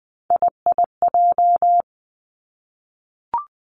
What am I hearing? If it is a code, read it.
Morse code, heard as a keyed tone, about 20 wpm, 700 Hz. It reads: IIJ
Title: Three Challenging Letter-Number Combinations 20wpm